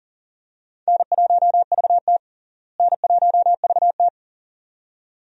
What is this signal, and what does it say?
Morse code, heard as a keyed tone, about 40 wpm, 700 Hz. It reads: N1VT N1VT
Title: USA Call Signs - 4 Characters - Repeated 2x - 40wpm